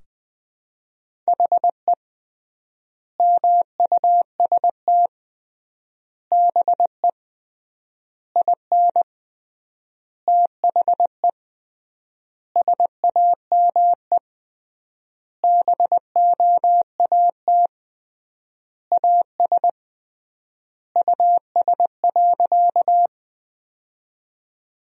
Morse code, heard as a keyed tone, about 20 wpm, 700 Hz. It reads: HE MUST BE IN THE SAME BOAT AS US.
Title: 3x Word-Spacing - Sentences from Top 500 Words - 20wpm